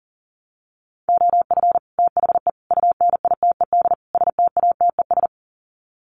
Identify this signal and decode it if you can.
Morse code, heard as a keyed tone, about 40 wpm, 700 Hz. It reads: OF THE UNITED STATES
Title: Sets of 4 Words 40wpm